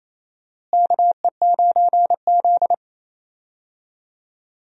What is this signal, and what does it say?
Morse code, heard as a keyed tone, about 28 wpm, 700 Hz. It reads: KE9Z
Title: USA Call Signs - 4 Characters - Repeated 2x - 28wpm